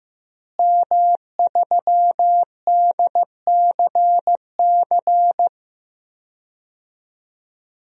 Morse code, heard as a keyed tone, about 15 wpm, 700 Hz. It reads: M3DCC
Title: DXCC-100 Call Signs - 5 Characters - Repeated 3x - 15wpm